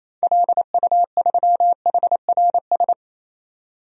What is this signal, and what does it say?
Morse code, heard as a keyed tone, about 28 wpm, 700 Hz. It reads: LU3HRS
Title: DXCC-100 Call Signs - 6 Characters - 28wpm